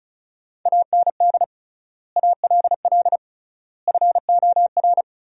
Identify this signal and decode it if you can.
Morse code, heard as a keyed tone, about 36 wpm, 700 Hz. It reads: AND ALL FOR
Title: 3x Word-Spacing - Sentences from Top 100 Words - 35wpm